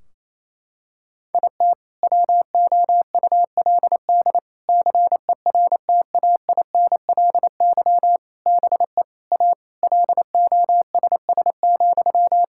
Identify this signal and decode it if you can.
Morse code, heard as a keyed tone, about 28 wpm, 700 Hz. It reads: IT WOULD CERTAINLY BE A LOSS,
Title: Sentences from Top 1600 Words 28wpm